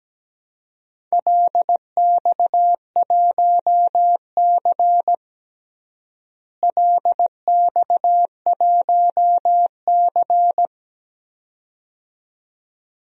Morse code, heard as a keyed tone, about 17 wpm, 700 Hz. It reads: LX1C LX1C
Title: DXCC-100 Call Signs - 4 Characters - Repeated 2x - 17wpm